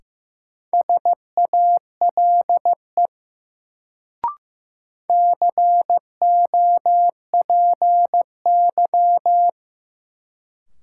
Morse code, heard as a keyed tone, about 15 wpm, 700 Hz. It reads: SALE COPY
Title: Sentences from Top 1400 Words - New Word Drill Alt - 15wpm